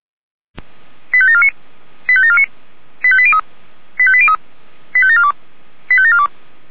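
Five-tone selective call (ZVEI), heard as quick runs of five tones, tones of about 70 ms. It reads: 85749 85749 85803 85803 85742 85742
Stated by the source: Five-tone ZVEI selective call received on an SDR.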